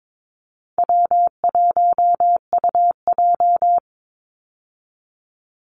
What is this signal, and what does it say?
Morse code, heard as a keyed tone, about 22 wpm, 700 Hz. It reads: W1UJ
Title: Call Signs Repeated 3x - 22wpm